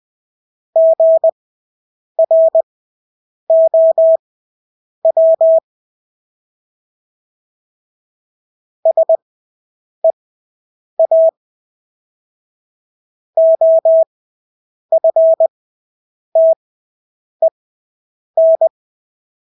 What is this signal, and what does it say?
Morse code, heard as a keyed tone, about 20 wpm, spaced as for 8 wpm, 650 Hz. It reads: GROW SEA OFTEN
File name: Engelske ord 20/8 wpm